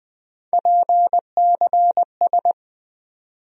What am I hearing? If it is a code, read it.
Morse code, heard as a keyed tone, about 20 wpm, 700 Hz. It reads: PCS
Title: USA Call Sign Postfixes - 3 Letters 20wpm